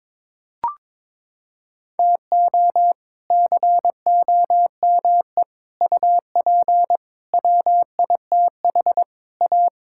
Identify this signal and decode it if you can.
Morse code, heard as a keyed tone, about 22 wpm, 700 Hz. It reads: TO COME UP WITH A
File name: practice / 22wpm / Sets of 7 Words - 22wpm